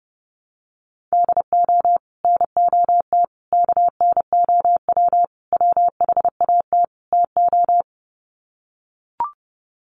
Morse code, heard as a keyed tone, about 30 wpm, 700 Hz. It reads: DO NOT KNOW WHAT TO
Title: Sets of 5 Words 30wpm